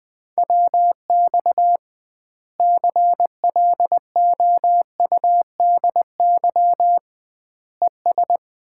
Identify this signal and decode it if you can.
Morse code, heard as a keyed tone, about 20 wpm, 700 Hz. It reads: WX CLOUDY ES
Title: Common QSO Phrases - 2x Word-Spacing - 20wpm